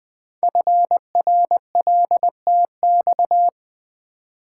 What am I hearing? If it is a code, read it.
Morse code, heard as a keyed tone, about 20 wpm, 700 Hz. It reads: FRLTX
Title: Code Groups - 5 Letters - 20wpm